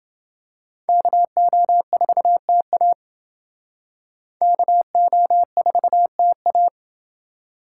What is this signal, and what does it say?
Morse code, heard as a keyed tone, about 28 wpm, 700 Hz. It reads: KO4TA KO4TA